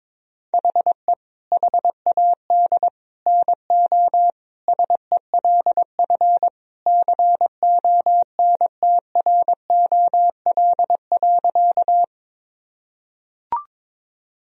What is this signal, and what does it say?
Morse code, heard as a keyed tone, about 22 wpm, 700 Hz. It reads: HE HAD NO SELF CONTROL.
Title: Sentences from Top 1000 Words - Review Part 3 - 22wpm